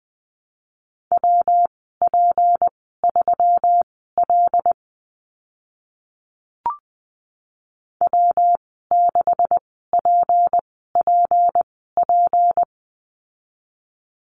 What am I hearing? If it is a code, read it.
Morse code, heard as a keyed tone, about 20 wpm, spaced as for 14 wpm, 700 Hz. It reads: WP3L W6PPP
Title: 154 Lesson 25g - Letter p - Call Signs 20wpm